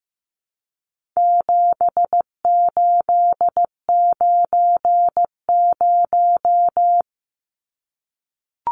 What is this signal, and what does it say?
Morse code, heard as a keyed tone, about 15 wpm, 700 Hz. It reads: 7890
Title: Numbers - 4-Digit 15wpm